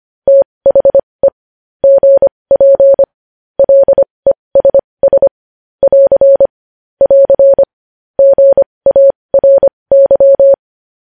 Morse code, heard as a keyed tone, about 25 wpm, spaced as for 20 wpm, 550 Hz. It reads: THE GP LESS <AR> <AR> GARY
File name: podcast / News headlines-1x-25x20